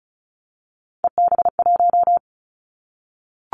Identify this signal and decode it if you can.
Morse code, heard as a keyed tone, about 34 wpm, 700 Hz. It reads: EB1
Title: Callsign Prefixes for Top 25 DXCC - Repeated 3x - 35wpm